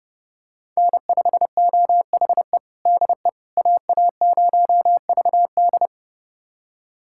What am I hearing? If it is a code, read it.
Morse code, heard as a keyed tone, about 30 wpm, 700 Hz. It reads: N5OHE DE AA0VD
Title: USA Callsign DE Callsign - 5 Characters - 30wpm